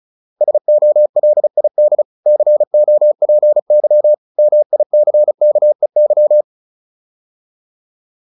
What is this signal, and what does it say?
Morse code, heard as a keyed tone, about 35 wpm, 600 Hz. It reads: SOLID COPY MICKEY
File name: QSO310 35